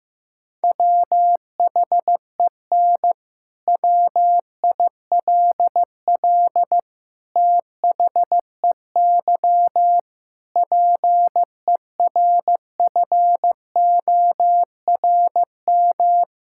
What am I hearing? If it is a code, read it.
Morse code, heard as a keyed tone, about 15 wpm, 700 Hz. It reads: WHEN WILL THEY PERFORM
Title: Sentences from Top 1000 Words - Review Part 5 - 15wpm